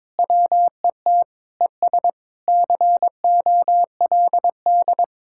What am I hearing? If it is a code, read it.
Morse code, heard as a keyed tone, about 22 wpm, 700 Hz. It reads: WET ES COLD
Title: Speed-Racing Common QSO Phrases - 22wpm